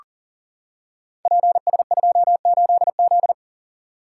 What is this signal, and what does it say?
Morse code, heard as a keyed tone, about 41 wpm, 690 Hz. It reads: PS28Z
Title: Code Groups - 5 Letter:Numbers - 40wpm